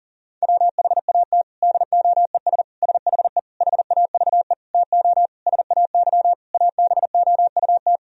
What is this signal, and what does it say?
Morse code, heard as a keyed tone, about 40 wpm, 700 Hz. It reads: WHAT DOES SHE HAVE TO SAY ABOUT
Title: Sentences from Top 1000 Words - Review Part 3 - 40wpm